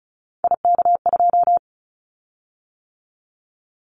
Morse code, heard as a keyed tone, about 36 wpm, 700 Hz. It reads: IK2
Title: Callsign Prefixes for Top 25 DXCC - 35wpm